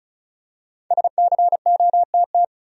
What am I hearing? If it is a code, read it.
Morse code, heard as a keyed tone, about 35 wpm, 700 Hz. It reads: SCOTT
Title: USA Common Names - Max 5 Letters - 35wpm